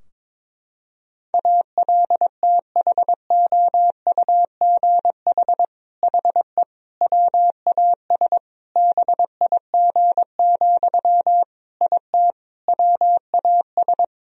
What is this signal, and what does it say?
Morse code, heard as a keyed tone, about 22 wpm, 700 Hz. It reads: ALTHOUGH HE WAS BIG, IT WAS
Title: Sentences from Top 1000 Words - Review Part 3 - 22wpm